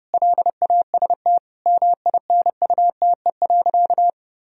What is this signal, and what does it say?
Morse code, heard as a keyed tone, about 30 wpm, 700 Hz. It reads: LAST MINUTE.
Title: Sentences from Top 1000 Words - Review Part 2 - 30wpm